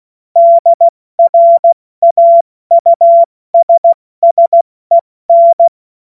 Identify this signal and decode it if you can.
Morse code, heard as a keyed tone, about 16 wpm, 680 Hz. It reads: DRAUSSEN
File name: Morsekurs / Witze / Witz 31. Januar 2023 70 F80